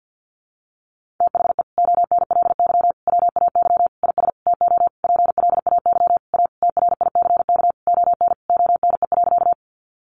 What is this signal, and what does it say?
Morse code, heard as a keyed tone, about 48 wpm, 700 Hz. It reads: THE ONLY WAY IS TO PLAY A TRICK ON ONE.
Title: Sentences from Top 1500 Words 50wpm